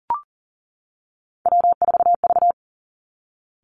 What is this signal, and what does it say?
Morse code, heard as a keyed tone, about 38 wpm, 700 Hz. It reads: W4V
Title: Three Challenging Letter-Number Combinations 40wpm